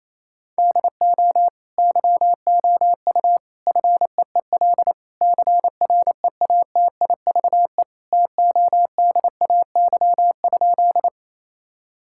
Morse code, heard as a keyed tone, about 27 wpm, 700 Hz. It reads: DO YOU FEEL CREATIVE TODAY?
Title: Sentences from Top 1000 Words - Review Part 5 - 28wpm